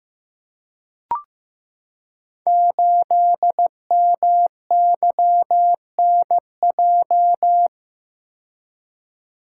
Morse code, heard as a keyed tone, about 15 wpm, 700 Hz. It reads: 8MYNJ
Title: Code Groups - 5 Letter:Numbers - 15wpm